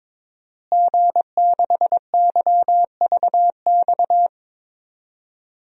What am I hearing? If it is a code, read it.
Morse code, heard as a keyed tone, about 22 wpm, 700 Hz. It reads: G6YVX
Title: DXCC-100 Call Signs - 5 Characters - Repeated 3x - 22wpm